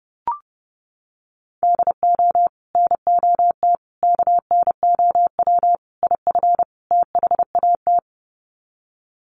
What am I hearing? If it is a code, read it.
Morse code, heard as a keyed tone, about 30 wpm, 700 Hz. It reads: DO NOT KNOW IF THAT
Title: Sets of 5 Words Encore2 30wpm